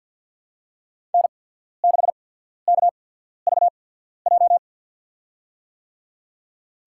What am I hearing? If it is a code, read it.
Morse code, heard as a keyed tone, about 54 wpm, spaced as for 14 wpm, 700 Hz. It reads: N6KVJ
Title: USA Call Signs - 5 Characters - 8x Character Spacing - 50wpm